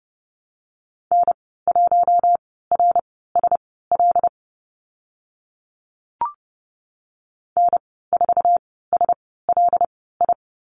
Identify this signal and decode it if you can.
Morse code, heard as a keyed tone, about 30 wpm, spaced as for 17 wpm, 700 Hz. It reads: N1RSL N4SLI
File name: Lesson 14g - Number 2 - Call Signs 30wpm